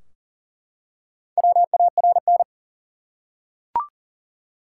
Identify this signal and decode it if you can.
Morse code, heard as a keyed tone, about 40 wpm, 700 Hz. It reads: WARN
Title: Four Letter Words - 40wpm